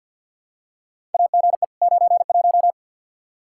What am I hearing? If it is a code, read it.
Morse code, heard as a keyed tone, about 47 wpm, 700 Hz. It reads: AGE 91
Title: Practice QSO - Exchange 3 - More Info - 50wpm